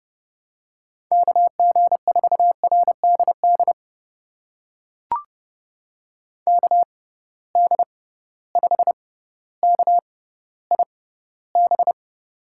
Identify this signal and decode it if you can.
Morse code, heard as a keyed tone, about 32 wpm, spaced as for 16 wpm, 700 Hz. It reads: KG4RDD KD5KIB
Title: USA Call Signs - 6 Characters - 6x Character Spacing - 30wpm